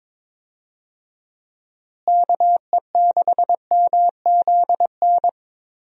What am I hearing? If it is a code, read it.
Morse code, heard as a keyed tone, about 21 wpm, 700 Hz. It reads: KE6MZN
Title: USA Call Signs - 6 Characters - Repeated 3x - 22wpm